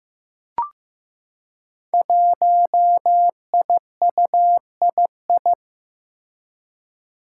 Morse code, heard as a keyed tone, about 15 wpm, 700 Hz. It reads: 1IUII
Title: Five Challenging Letter-Number Combinations 15wpm